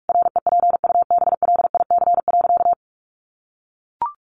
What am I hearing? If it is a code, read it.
Morse code, heard as a keyed tone, about 44 wpm, 700 Hz. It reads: REPUBLIC.
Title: Sentences from Top 1500 Words 45wpm